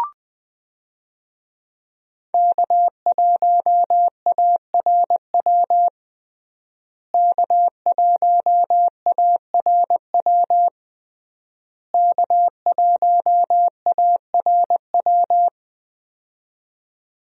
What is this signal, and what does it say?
Morse code, heard as a keyed tone, about 20 wpm, 700 Hz. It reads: K1ARW K1ARW K1ARW
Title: USA Call Signs - 5 Characters - Repeated 3x - 20wpm